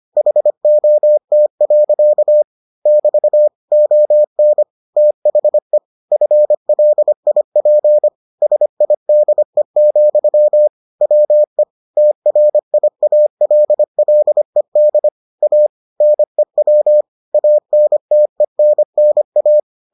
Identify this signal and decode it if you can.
Morse code, heard as a keyed tone, about 25 wpm, 600 Hz. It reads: HOT. <BT> ON THE FLIP SIDE, WE TRIALLED A NEW ANTENNA